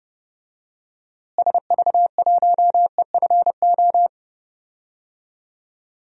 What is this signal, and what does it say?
Morse code, heard as a keyed tone, about 30 wpm, 700 Hz. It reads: SV1EFO
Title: DXCC-100 Call Signs - 6 Characters - Repeated 3x - 30wpm